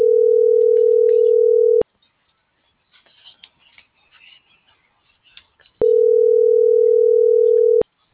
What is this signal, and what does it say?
Ringback tone (North America), 440 + 480 Hz.